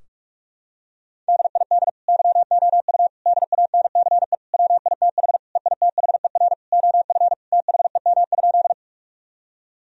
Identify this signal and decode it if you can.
Morse code, heard as a keyed tone, about 46 wpm, 700 Hz. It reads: DID YOU DANCE WITH EITHER OF THEM?
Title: Sentences from Top 1000 Words - Review Part 4 - 45wpm